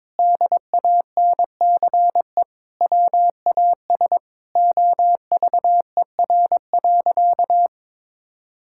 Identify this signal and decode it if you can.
Morse code, heard as a keyed tone, about 23 wpm, 700 Hz. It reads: DANCE WAS OVER.
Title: Sentences from Top 1000 Words - Review Part 6 - 22wpm